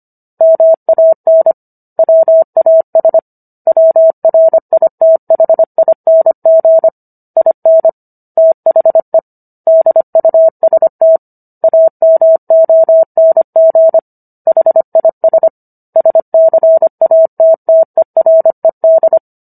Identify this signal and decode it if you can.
Morse code, heard as a keyed tone, about 25 wpm, 650 Hz. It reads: MAN WAS WRITHING IN THE DUST AMONG HIS SCATTERED